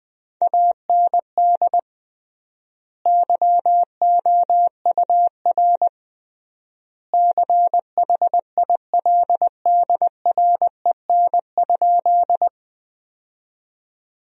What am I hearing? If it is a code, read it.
Morse code, heard as a keyed tone, about 20 wpm, 700 Hz. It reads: AND YOUR CHILDREN?
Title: 3x Word-Spacing - Sentences from Top 300 Words - 20wpm